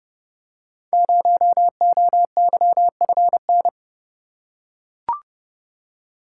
Morse code, heard as a keyed tone, about 29 wpm, 700 Hz. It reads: 0OYFN
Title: Code Groups - 5 Letter:Numbers - 30wpm